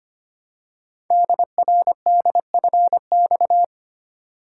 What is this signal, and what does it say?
Morse code, heard as a keyed tone, about 25 wpm, 700 Hz. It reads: DRDFX
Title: Code Groups - 5 Letters - 25wpm